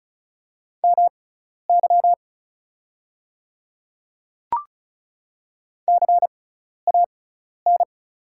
Morse code, heard as a keyed tone, about 34 wpm, spaced as for 12 wpm, 700 Hz. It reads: MY CAN
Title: 6x Character Spacing - Top 100 Words - 35wpm